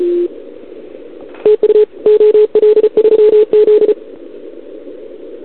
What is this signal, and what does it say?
Morse code, heard as a keyed tone, about 34 wpm, 440 Hz. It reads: TU OL3Z
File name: CQWWCW2021 / OL3Z